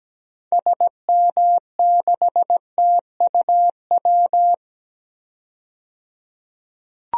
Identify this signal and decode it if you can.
Morse code, heard as a keyed tone, about 17 wpm, 700 Hz. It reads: SM6TUW
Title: DXCC-100 Call Signs - 6 Characters - 17wpm